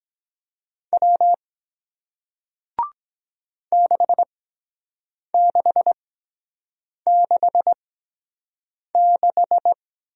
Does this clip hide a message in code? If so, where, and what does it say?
Yes: Morse code from 0.9 to 9.7 s, heard as a keyed tone, about 23 wpm, spaced as for 7 wpm, 700 Hz. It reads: W 6666